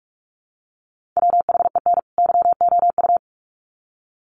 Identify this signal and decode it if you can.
Morse code, heard as a keyed tone, about 44 wpm, 700 Hz. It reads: WHEN YOU
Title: Sets of 2 Words 45wpm